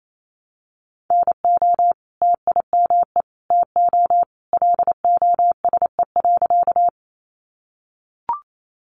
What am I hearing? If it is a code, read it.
Morse code, heard as a keyed tone, about 28 wpm, 700 Hz. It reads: NO TIME TO LOSE.
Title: Sentences from Top 1100 Words 28wpm